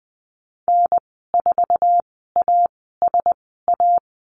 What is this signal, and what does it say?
Morse code, heard as a keyed tone, about 20 wpm, spaced as for 14 wpm, 700 Hz. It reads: N4ASA
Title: 082 Lesson 14g - Number 2 - Call Signs 20wpm